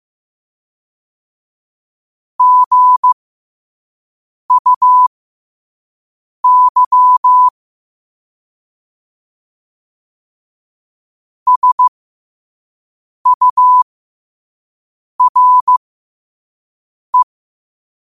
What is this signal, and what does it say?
Morse code, heard as a keyed tone, about 15 wpm, spaced as for 5 wpm, 1000 Hz. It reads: GUY SURE
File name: qotdfiles / 5wpm